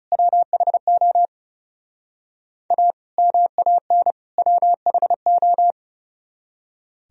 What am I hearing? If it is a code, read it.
Morse code, heard as a keyed tone, about 37 wpm, 700 Hz. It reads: WHO A MAN WHO
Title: Speed-Racing - Sets of 3 Words 30wpm